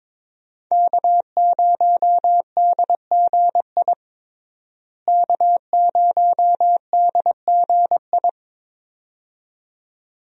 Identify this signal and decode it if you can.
Morse code, heard as a keyed tone, about 22 wpm, 700 Hz. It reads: K0DGI K0DGI
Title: USA Call Signs - 5 Characters - Repeated 2x - 22wpm